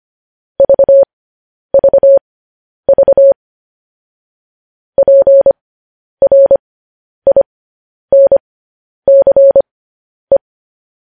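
Morse code, heard as a keyed tone, about 25 wpm, spaced as for 10 wpm, 550 Hz. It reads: VVV PRINCE
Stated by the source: Morse code news headlines practice at 10 wpm.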